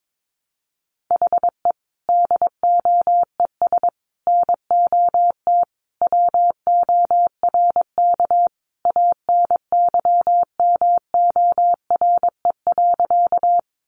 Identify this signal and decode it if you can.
Morse code, heard as a keyed tone, about 22 wpm, 700 Hz. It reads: HE DOES NOT WORK ANYMORE.
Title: Additional Easy Phrases 22wpm